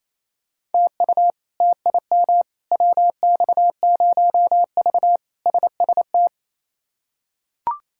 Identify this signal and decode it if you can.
Morse code, heard as a keyed tone, about 28 wpm, 700 Hz. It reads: TU TIM WX0V SST
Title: K1USN SST QSO Practice - 28wpm